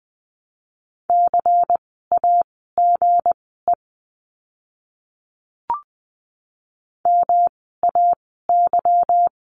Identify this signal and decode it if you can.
Morse code, heard as a keyed tone, about 20 wpm, spaced as for 14 wpm, 700 Hz. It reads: CAGE MAY
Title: 160 Lesson 26f - Letter g - Words 20wpm